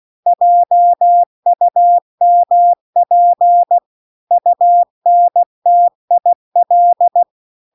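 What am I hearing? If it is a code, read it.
Morse code, heard as a keyed tone, about 16 wpm, 700 Hz. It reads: JUMP UNTIL
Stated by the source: Morse practice text at 16 WPM.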